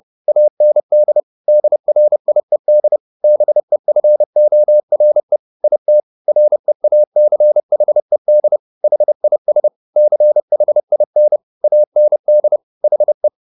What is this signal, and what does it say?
Morse code, heard as a keyed tone, about 30 wpm, 600 Hz. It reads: AND DRIED BEFORE IT REACHED HIS CHIN AND HE
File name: the old man and the sea / the old man and the sea 1 30wpm